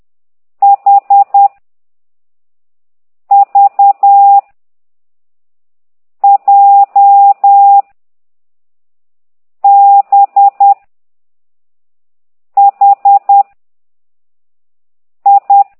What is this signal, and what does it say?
Morse code, heard as a keyed tone, about 10 wpm, 800 Hz. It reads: H V J B H I
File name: ct2idw / exerc13